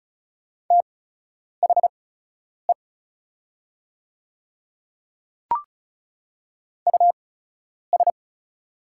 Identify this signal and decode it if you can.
Morse code, heard as a keyed tone, about 36 wpm, spaced as for 10 wpm, 700 Hz. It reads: THE US
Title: 8x Character Spacing - Top 200 Words - 35wpm